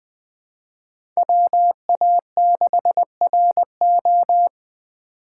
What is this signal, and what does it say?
Morse code, heard as a keyed tone, about 20 wpm, 700 Hz. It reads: WA6RO
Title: USA Call Signs - 3x Character Spacing - 20wpm